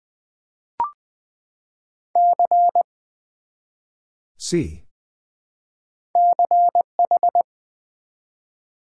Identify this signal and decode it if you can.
Morse code, heard as a keyed tone, about 20 wpm, 700 Hz. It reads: C CH
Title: Build A Word - 1 Letter At A Time - 5-Letter Words - 20wpm